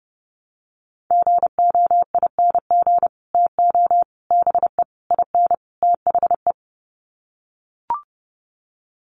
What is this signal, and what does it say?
Morse code, heard as a keyed tone, about 30 wpm, 700 Hz. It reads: GOING TO BE IN THE
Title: Sets of 5 Words Encore 30wpm